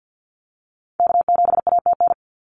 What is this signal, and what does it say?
Morse code, heard as a keyed tone, about 50 wpm, 700 Hz. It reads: K7ATN
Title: NA SOTA Activator Callsigns 50wpm